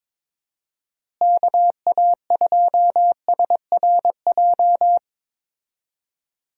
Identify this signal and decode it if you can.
Morse code, heard as a keyed tone, about 21 wpm, 700 Hz. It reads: KA2SRJ
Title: USA Call Signs - 6 Characters - Repeated 3x - 22wpm